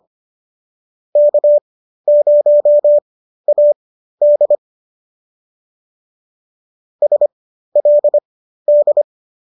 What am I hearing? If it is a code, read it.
Morse code, heard as a keyed tone, about 25 wpm, spaced as for 13 wpm, 590 Hz. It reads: K0AD SLD
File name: qso213 13